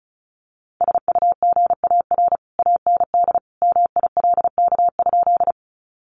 Morse code, heard as a keyed tone, about 34 wpm, 700 Hz. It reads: SUGAR AND MILK?
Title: Sentences from Top 1500 Words 35wpm